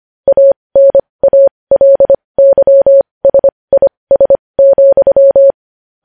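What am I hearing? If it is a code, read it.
Morse code, heard as a keyed tone, about 25 wpm, spaced as for 20 wpm, 550 Hz. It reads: ANALYSIS,
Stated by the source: Morse practice of news headlines at 20 wpm.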